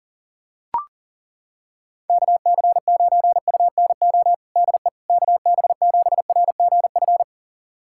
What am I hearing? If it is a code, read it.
Morse code, heard as a keyed tone, about 40 wpm, 700 Hz. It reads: KC9UNO DE KB7RGF
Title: USA Callsign DE Callsign 40wpm